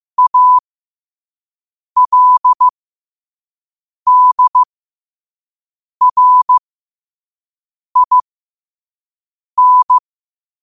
Morse code, heard as a keyed tone, about 15 wpm, spaced as for 5 wpm, 1000 Hz. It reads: ALDRIN